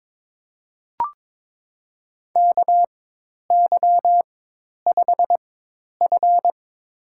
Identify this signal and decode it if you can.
Morse code, heard as a keyed tone, about 22 wpm, spaced as for 10 wpm, 700 Hz. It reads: KY5F
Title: USA Call Signs - 4x Character Spacing - 22wpm